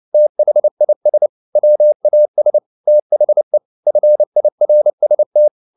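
Morse code, heard as a keyed tone, about 29 wpm, 600 Hz. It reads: THIS WAS THE FIRST
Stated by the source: Morse code at 29 wpm, sending Ken Follett's novel Never in English, from CW med Gnister.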